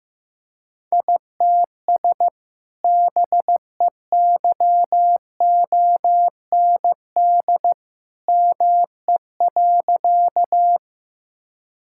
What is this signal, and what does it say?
Morse code, heard as a keyed tone, about 15 wpm, 700 Hz. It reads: ITS BEYOND ME.
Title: Sentences from Top 1000 Words - Review Part 6 - 15wpm